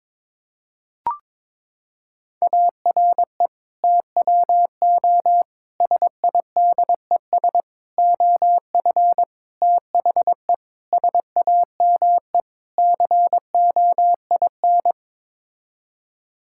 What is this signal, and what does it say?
Morse code, heard as a keyed tone, about 22 wpm, 700 Hz. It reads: ARE TWO SIDES OF THE SAME COIN
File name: practice / 22wpm / Sets of 7 Words - 22wpm